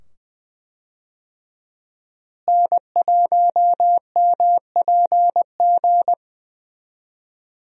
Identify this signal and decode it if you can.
Morse code, heard as a keyed tone, about 20 wpm, 700 Hz. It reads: N1MPG
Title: USA Call Signs - 5 Characters - Repeated 2x - 20wpm